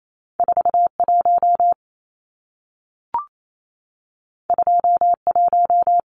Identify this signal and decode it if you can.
Morse code, heard as a keyed tone, about 28 wpm, 700 Hz. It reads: 41 21
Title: Numbers - 2-Digit 28wpm